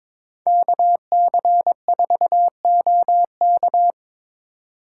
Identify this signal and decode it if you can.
Morse code, heard as a keyed tone, about 22 wpm, 700 Hz. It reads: KC4OK
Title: USA Call Signs - 5 Characters - 4x Character Spacing - 22wpm